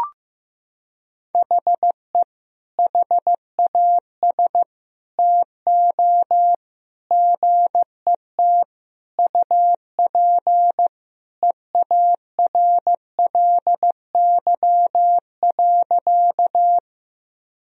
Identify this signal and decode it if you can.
Morse code, heard as a keyed tone, about 15 wpm, 700 Hz. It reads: HE HAS TO GET UP EARLY.
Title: Sentences from Top 1000 Words - Review Part 2 - 15wpm